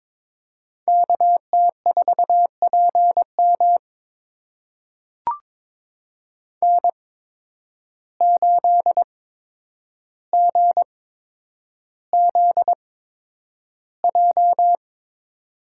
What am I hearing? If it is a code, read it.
Morse code, heard as a keyed tone, about 22 wpm, spaced as for 9 wpm, 700 Hz. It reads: KT4PM N8GZJ
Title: USA Call Signs - 5 Characters - 8x Character Spacing - 22wpm